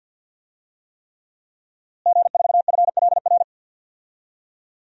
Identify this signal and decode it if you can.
Morse code, heard as a keyed tone, about 53 wpm, 700 Hz. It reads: G4FLR
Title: DXCC-100 Call Signs - 5 Characters - Repeated 3x - 50wpm